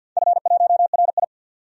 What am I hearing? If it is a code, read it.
Morse code, heard as a keyed tone, about 51 wpm, 700 Hz. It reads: F1RI
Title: DXCC-100 Call Signs - 4 Characters - 50wpm